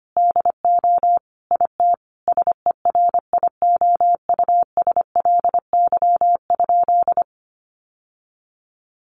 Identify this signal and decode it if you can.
Morse code, heard as a keyed tone, about 25 wpm, 700 Hz. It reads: DO IT SERIOUSLY?
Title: Sentences from Top 1200 Words 25wpm